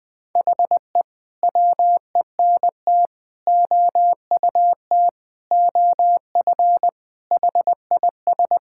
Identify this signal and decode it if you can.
Morse code, heard as a keyed tone, about 20 wpm, 700 Hz. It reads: HE WENT OUT OF HIS
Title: Sentences with 6-Word Phrases - 20wpm